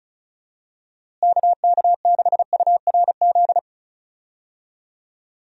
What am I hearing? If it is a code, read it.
Morse code, heard as a keyed tone, about 36 wpm, 700 Hz. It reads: KK6URZ
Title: USA Call Signs - 6 Characters - Repeated 3x - 35wpm